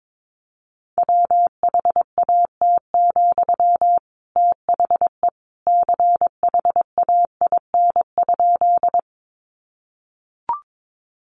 Morse code, heard as a keyed tone, about 22 wpm, 700 Hz. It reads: WHAT, THE CHAIN?
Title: Sentences from Top 1500 Words 22wpm